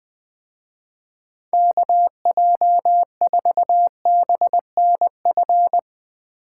Morse code, heard as a keyed tone, about 20 wpm, 700 Hz. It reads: KJ4BNF